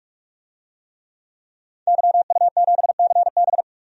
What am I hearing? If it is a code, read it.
Morse code, heard as a keyed tone, about 45 wpm, 700 Hz. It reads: YU7CB
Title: DXCC-100 Call Signs - 5 Characters - 45wpm